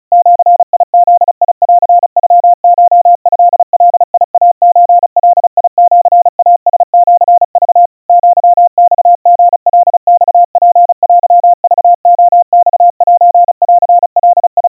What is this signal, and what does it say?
Morse code (Japanese Wabun code), heard as a keyed tone, about 36 wpm, 700 Hz. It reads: ジブンノコトガイソガシイラシク アマリカマツテクレマセンガ
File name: kawa / Morse sample / kamyu 130 cpm 700 Hz Crispy